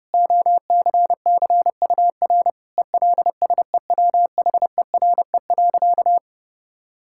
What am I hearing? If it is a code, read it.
Morse code, heard as a keyed tone, about 28 wpm, 700 Hz. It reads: OCCUR ELSEWHERE.
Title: Sentences from Top 1600 Words 30wpm